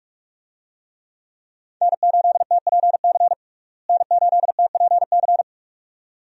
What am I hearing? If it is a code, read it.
Morse code, heard as a keyed tone, about 46 wpm, 700 Hz. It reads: N8TPC N8TPC
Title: USA Call Signs - 5 Characters - Repeated 2x - 45wpm